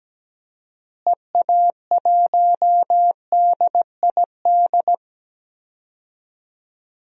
Morse code, heard as a keyed tone, about 17 wpm, 700 Hz. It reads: EA1DID
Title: DXCC-100 Call Signs - 6 Characters - Repeated 2x - 17wpm